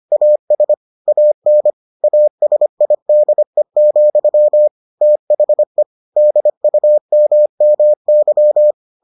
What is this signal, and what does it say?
Morse code, heard as a keyed tone, about 25 wpm, 600 Hz. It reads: AS AN ASIDE, THE DUMMY